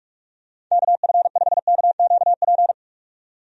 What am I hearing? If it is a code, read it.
Morse code, heard as a keyed tone, about 46 wpm, 700 Hz. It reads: KF5KQP
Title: USA Call Signs - 6 Characters - 45wpm